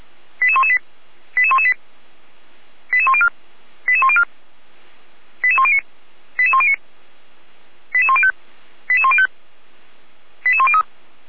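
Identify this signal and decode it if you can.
Five-tone selective call (ZVEI), heard as quick runs of five tones, tones of about 70 ms. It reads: 88118 88118 88104 88104 88109 88109 88105 88105 88103
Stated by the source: ZVEI five-tone call received on SDR, from the Feuerwehr Orlamünde fire brigade audio archive.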